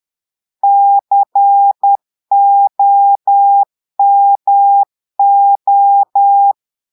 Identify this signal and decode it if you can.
Morse code, heard as a keyed tone, about 10 wpm, 800 Hz. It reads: COMO
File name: portugais texte en clair 2 vitesse 10 mots minutes